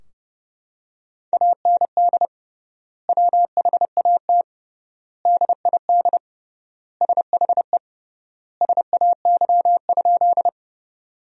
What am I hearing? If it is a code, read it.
Morse code, heard as a keyed tone, about 30 wpm, 700 Hz. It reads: AND WHAT DID SHE SAY?
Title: 3x Word-Spacing - Sentences from Top 200 Words - 30wpm